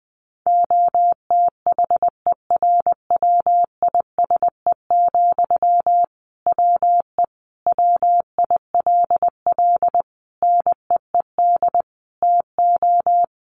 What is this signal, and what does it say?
Morse code, heard as a keyed tone, about 20 wpm, 700 Hz. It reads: OTHERWISE, WE WILL NEED TO
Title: Sentences from Top 1500 Words 20wpm